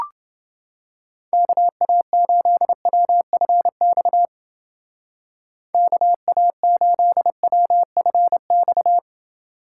Morse code, heard as a keyed tone, about 28 wpm, 700 Hz. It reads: KA8WFX KA8WFX